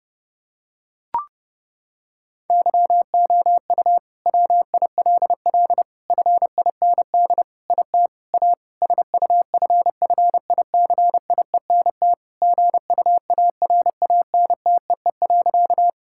Morse code, heard as a keyed tone, about 30 wpm, 700 Hz. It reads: YOU WILL FIND IT A SUFFICIENT GUARANTEE.
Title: Sentences from Top 1600 Words 30wpm